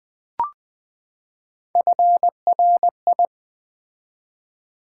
Morse code, heard as a keyed tone, about 20 wpm, 700 Hz. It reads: FRI
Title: Build A Word - 3 Letters At A Time - 8-Letter Words - 20wpm